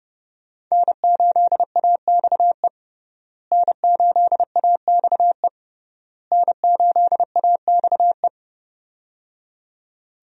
Morse code, heard as a keyed tone, about 30 wpm, 700 Hz. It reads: N8AXE N8AXE N8AXE
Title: USA Call Signs - 5 Characters - Repeated 3x - 30wpm